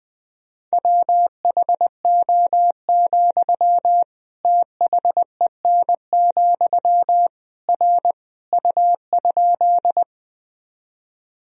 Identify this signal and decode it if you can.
Morse code, heard as a keyed tone, about 20 wpm, 700 Hz. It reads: WHO, THEN, R U?
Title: Sentences from Top 100 Words and Abbreviations 20wpm